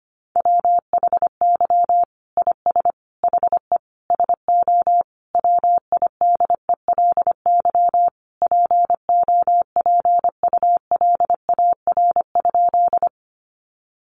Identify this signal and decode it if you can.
Morse code, heard as a keyed tone, about 25 wpm, 700 Hz. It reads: WHY IS HE SO WIDELY POPULAR?
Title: Sentences from Top 1200 Words 25wpm